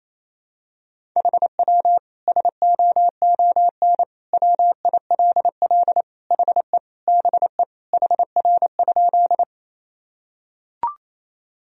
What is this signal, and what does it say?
Morse code, heard as a keyed tone, about 28 wpm, 700 Hz. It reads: HW SOON WILL HE BE HR?
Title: Sentences from Top 400 Words and Abbreviations 28wpm